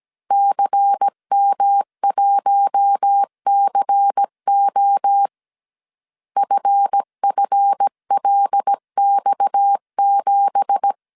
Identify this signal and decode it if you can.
Morse code, heard as a keyed tone, about 17 wpm, 800 Hz. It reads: CM1CO FFLX7
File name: CW / DET / 1 9 1500C S A 16wpm 2828